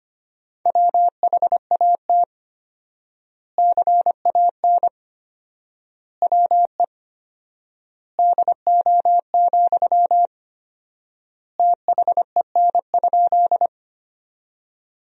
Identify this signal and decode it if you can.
Morse code, heard as a keyed tone, about 25 wpm, 700 Hz. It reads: WHAT CAN WE DO, THEN?
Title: 4x Word-Spacing - Sentences from Top 100 Words - 25wpm